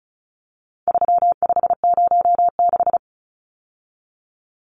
Morse code, heard as a keyed tone, about 34 wpm, 700 Hz. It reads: 3506
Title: Numbers - 4-Digit 35wpm